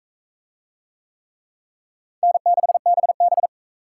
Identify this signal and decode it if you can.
Morse code, heard as a keyed tone, about 40 wpm, 700 Hz. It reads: N6BB